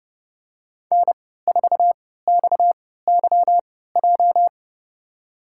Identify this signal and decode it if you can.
Morse code, heard as a keyed tone, about 29 wpm, spaced as for 17 wpm, 700 Hz. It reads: N4XYJ
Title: USA Call Signs - 5 Characters - 3x Character Spacing - 30wpm